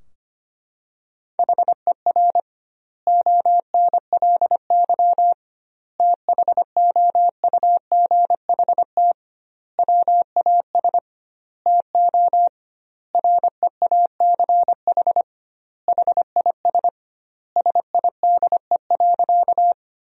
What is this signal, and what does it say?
Morse code, heard as a keyed tone, about 25 wpm, 700 Hz. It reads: HER ONLY THOUGHT WAS TO REACH HIS SIDE.
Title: 2x Word-Spacing - Sentences from Top 400 Words - 25wpm